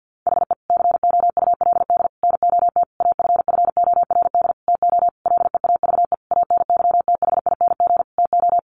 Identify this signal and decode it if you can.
Morse code, heard as a keyed tone, about 49 wpm, 700 Hz. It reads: HE COULD NOT AFFORD TO LEAVE ANYTHING TO
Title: Sentences from Top 1100 Words 50wpm